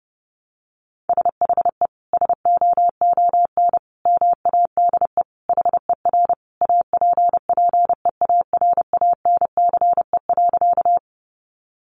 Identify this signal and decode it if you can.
Morse code, heard as a keyed tone, about 30 wpm, 700 Hz. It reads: SHE SOON MADE HER APPEARANCE.
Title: Sentences from Top 1100 Words 30wpm